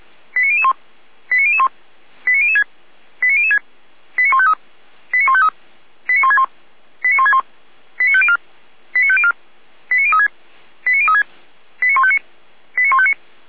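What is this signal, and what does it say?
Five-tone selective call (ZVEI), heard as quick runs of five tones, tones of about 80 ms. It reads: 89001 89001 89006 89006 89153 89153 89161 89161 89504 89504 89036 89036 89159 89159
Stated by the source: ZVEI five-tone selective call received by SDR, feuerwehr-orlamuende.org (fire brigade Orlamünde) recording.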